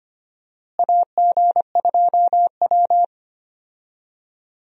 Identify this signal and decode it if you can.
Morse code, heard as a keyed tone, about 25 wpm, 700 Hz. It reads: AG2W
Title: USA Call Signs - 4 Characters - Repeated 3x - 25wpm